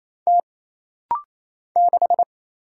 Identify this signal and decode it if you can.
Morse code, heard as a keyed tone, about 28 wpm, 700 Hz. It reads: T 6
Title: Single Letter-Number - Rapid-Fire 28wpm